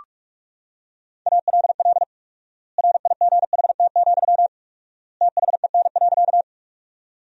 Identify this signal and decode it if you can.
Morse code, heard as a keyed tone, about 42 wpm, 700 Hz. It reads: ALL RIGHT, THEN.
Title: 4x Word-Spacing - Sentences from Top 200 Words - 45wpm